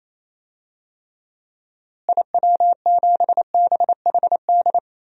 Morse code, heard as a keyed tone, about 28 wpm, 700 Hz. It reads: IW7BHD
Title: DXCC-100 Call Signs - 6 Characters - Repeated 3x - 28wpm